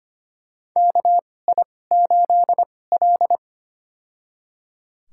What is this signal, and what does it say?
Morse code, heard as a keyed tone, about 25 wpm, spaced as for 18 wpm, 700 Hz. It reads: KI8L
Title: USA Call Signs - 4 Characters - 2x Character Spacing - 25wpm